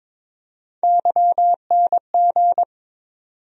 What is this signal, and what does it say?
Morse code, heard as a keyed tone, about 22 wpm, 700 Hz. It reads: YNG
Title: USA Call Sign Postfixes - 3 Letters 22wpm